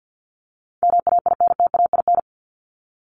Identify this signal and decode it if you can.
Morse code, heard as a keyed tone, about 50 wpm, 700 Hz. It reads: MAINTAIN
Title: Top 501-750 Words 50wpm